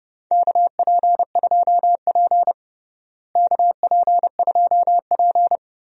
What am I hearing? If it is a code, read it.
Morse code, heard as a keyed tone, about 30 wpm, 700 Hz. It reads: KP2P KP2P
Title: DXCC-100 Call Signs - 4 Characters - Repeated 2x - 30wpm